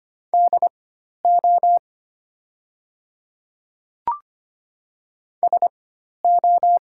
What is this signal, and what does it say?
Morse code, heard as a keyed tone, about 25 wpm, spaced as for 12 wpm, 700 Hz. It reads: DO SO
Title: 4x Character Spacing - Top 100 Words - 25wpm